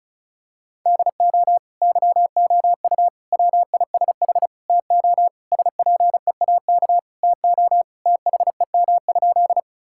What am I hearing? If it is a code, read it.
Morse code, heard as a keyed tone, about 36 wpm, 700 Hz. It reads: DO YOU WISH TO SPEAK TO THEM?
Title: Sentences from Top 1600 Words 35wpm